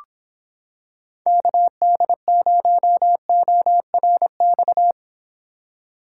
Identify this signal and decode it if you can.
Morse code, heard as a keyed tone, about 26 wpm, 700 Hz. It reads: KD0ORX